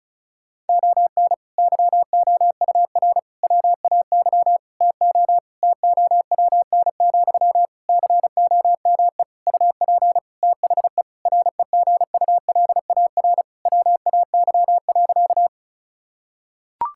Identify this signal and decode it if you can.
Morse code, heard as a keyed tone, about 36 wpm, 700 Hz. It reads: ON YOUR WAY TO TOWN, COME UP THE REGULAR WAY.
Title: Sentences from Top 1000 Words - Review Part 6 - 35wpm